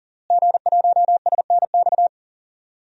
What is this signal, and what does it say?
Morse code, heard as a keyed tone, about 40 wpm, 700 Hz. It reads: G1SNX
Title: DXCC-100 Call Signs - 5 Characters - Repeated 3x - 40wpm